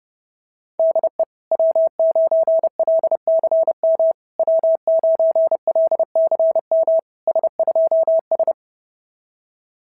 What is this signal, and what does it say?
Morse code, heard as a keyed tone, about 30 wpm, 660 Hz. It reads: DE W9LCM W9LCM S2S
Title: SOTA QSO Practice 30wpm